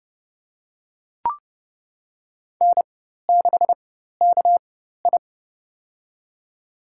Morse code, heard as a keyed tone, about 30 wpm, spaced as for 14 wpm, 700 Hz. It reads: N6KI
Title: USA Call Signs - 4 Characters - 4x Character Spacing - 30wpm